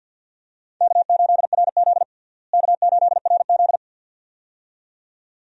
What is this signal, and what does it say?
Morse code, heard as a keyed tone, about 52 wpm, 700 Hz. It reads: K8RZ K8RZ
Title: USA Call Signs - 4 Characters - Repeated 2x - 50wpm